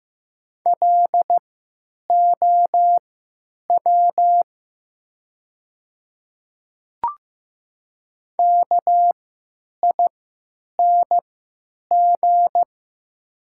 Morse code, heard as a keyed tone, about 15 wpm, spaced as for 9 wpm, 700 Hz. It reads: LOW KING
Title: 3x Character Spacing - Top 500 Words - 15wpm